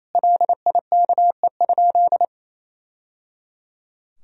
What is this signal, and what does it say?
Morse code, heard as a keyed tone, about 27 wpm, 700 Hz. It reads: LIKE?
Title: Sentences from Top 1000 Words - Review Part 3 - 28wpm